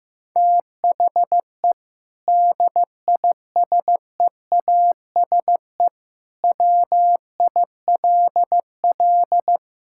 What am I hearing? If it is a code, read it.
Morse code, heard as a keyed tone, about 15 wpm, 700 Hz. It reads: THE DISEASE WILL
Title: Sentences from Top 1000 Words - Review Part 6 - 15wpm